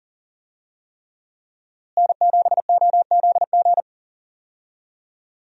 Morse code, heard as a keyed tone, about 37 wpm, 700 Hz. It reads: N7OZG
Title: USA Call Signs - 5 Characters - Repeated 3x - 40wpm